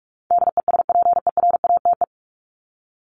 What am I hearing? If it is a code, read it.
Morse code, heard as a keyed tone, about 44 wpm, 700 Hz. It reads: DESPERATE
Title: Top 1751-2000 Words 45wpm